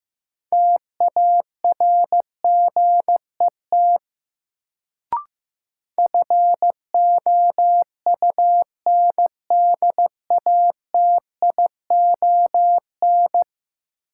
Morse code, heard as a keyed tone, about 15 wpm, 700 Hz. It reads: TARGET FOUNDATION
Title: Sentences from Top 1200 Words - New Word Drill - 15wpm